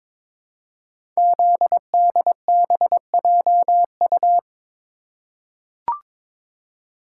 Morse code, heard as a keyed tone, about 23 wpm, 700 Hz. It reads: ZDBJU
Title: Code Groups - 5 Letters - 22wpm